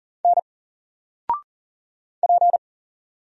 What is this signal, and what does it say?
Morse code, heard as a keyed tone, about 39 wpm, 700 Hz. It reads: N P
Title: Single Letter-Number 40wpm